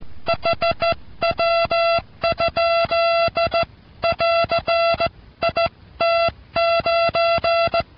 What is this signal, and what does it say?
Morse code, heard as a keyed tone, about 16 wpm, 680 Hz, with noise. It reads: HW? <AR> IT9